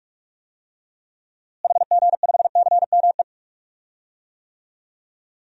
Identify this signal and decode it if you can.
Morse code, heard as a keyed tone, about 44 wpm, 700 Hz. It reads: HG5CME
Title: DXCC-100 Call Signs - 6 Characters - Repeated 3x - 45wpm